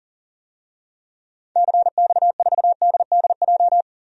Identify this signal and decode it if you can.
Morse code, heard as a keyed tone, about 37 wpm, 700 Hz. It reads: CX4DDJ
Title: DXCC-100 Call Signs - 6 Characters - 40wpm